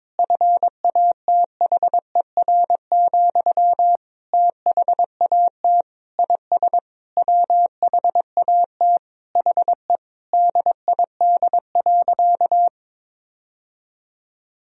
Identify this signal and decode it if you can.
Morse code, heard as a keyed tone, about 22 wpm, 700 Hz. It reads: FATHER, THAT IS WHAT HE DID.
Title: Sentences from Top 1000 Words - Review Part 4 - 22wpm